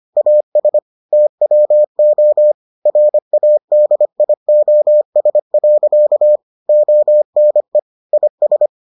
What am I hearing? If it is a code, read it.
Morse code, heard as a keyed tone, about 25 wpm, 600 Hz. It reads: AS TWO RADIOS. ONE IS